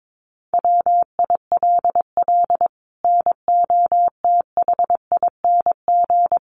Morse code, heard as a keyed tone, about 22 wpm, 700 Hz. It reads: WILL NOTHING
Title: Sentences from Top 1200 Words 22wpm